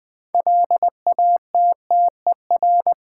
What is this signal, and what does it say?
Morse code, heard as a keyed tone, about 20 wpm, 700 Hz. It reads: LATTER
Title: Sentences from Top 1600 Words - New Word Drill - 20wpm